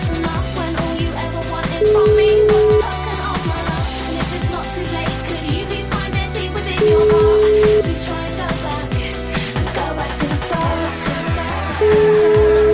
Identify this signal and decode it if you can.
Ringback tone (Europe, 425 Hz), with noise.